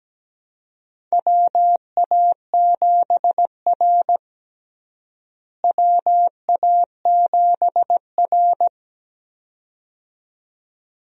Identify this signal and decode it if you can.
Morse code, heard as a keyed tone, about 17 wpm, 700 Hz. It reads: WA7R WA7R
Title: USA Call Signs - 4 Characters - Repeated 2x - 17wpm